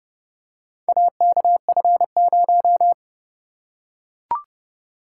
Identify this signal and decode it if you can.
Morse code, heard as a keyed tone, about 29 wpm, 700 Hz. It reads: AKF0